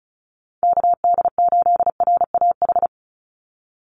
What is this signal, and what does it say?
Morse code, heard as a keyed tone, about 34 wpm, 700 Hz. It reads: KD8RAH
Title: USA Call Signs 35wpm